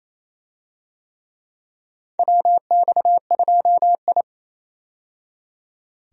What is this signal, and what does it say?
Morse code, heard as a keyed tone, about 28 wpm, 700 Hz. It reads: WX2I
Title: USA Call Signs - 4 Characters - 28wpm